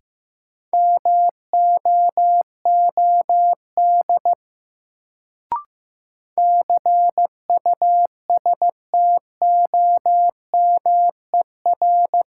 Morse code, heard as a keyed tone, about 15 wpm, 700 Hz. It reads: MOOD CUSTOMER
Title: Sentences from Top 1100 Words - New Word Drill Alt - 15wpm